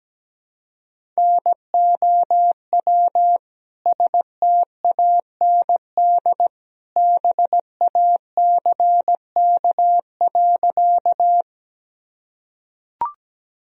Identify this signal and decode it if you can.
Morse code, heard as a keyed tone, about 17 wpm, 700 Hz. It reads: NOW STAND BACK.
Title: Sentences from Top 1000 Words - Review Part 1 - 17wpm